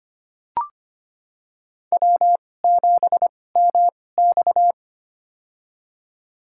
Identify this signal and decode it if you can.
Morse code, heard as a keyed tone, about 25 wpm, spaced as for 18 wpm, 700 Hz. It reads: W7MX
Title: USA Call Signs - 4 Characters - 2x Character Spacing - 25wpm